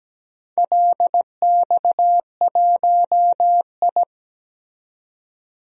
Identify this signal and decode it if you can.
Morse code, heard as a keyed tone, about 17 wpm, 700 Hz. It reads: LX1I
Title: DXCC-100 Call Signs - 4 Characters - 17wpm